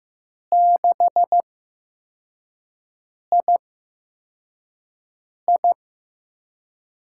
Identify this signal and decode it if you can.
Morse code, heard as a keyed tone, about 15 wpm, spaced as for 4 wpm, 700 Hz. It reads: 6II
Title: USA Call Signs - 4 Characters - 8x Character Spacing - 15wpm